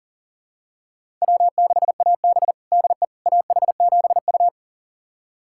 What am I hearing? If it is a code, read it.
Morse code, heard as a keyed tone, about 40 wpm, 700 Hz. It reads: W6AB DE AH7U
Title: USA Callsign DE Callsign - 4 Characters - 40wpm